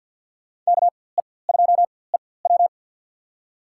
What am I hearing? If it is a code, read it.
Morse code, heard as a keyed tone, about 48 wpm, spaced as for 23 wpm, 700 Hz. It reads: KE2EW
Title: USA Call Signs - 5 Characters - 4x Character Spacing - 50wpm